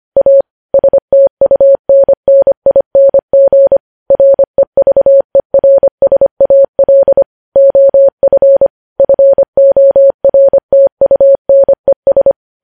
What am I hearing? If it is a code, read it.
Morse code, heard as a keyed tone, about 25 wpm, 550 Hz. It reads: A STUNNING REVERSAL OF FORTUNES
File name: podcast / News headlines-1x-25x25